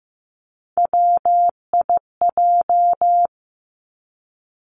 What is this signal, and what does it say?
Morse code, heard as a keyed tone, about 15 wpm, 700 Hz. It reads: WIJ
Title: Three Challenging Letter-Number Combinations 15wpm